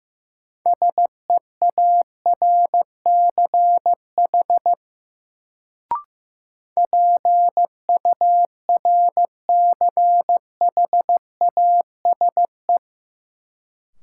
Morse code, heard as a keyed tone, about 16 wpm, 700 Hz. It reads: SEARCH PURCHASE
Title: Sentences from Top 800 Words - New Word Drill - 15wpm